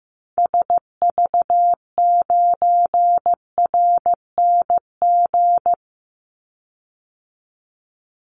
Morse code, heard as a keyed tone, about 15 wpm, 700 Hz. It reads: SV9RNG
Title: Call Signs Repeated 3x - 15wpm